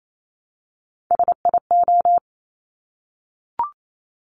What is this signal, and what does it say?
Morse code, heard as a keyed tone, about 28 wpm, 700 Hz. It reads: SIO
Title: Top Three Letter Combinations in Words 28wpm